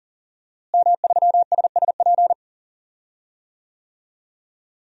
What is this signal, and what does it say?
Morse code, heard as a keyed tone, about 40 wpm, 700 Hz. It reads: M3SSP
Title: DXCC-100 Call Signs - 5 Characters - Repeated 3x - 40wpm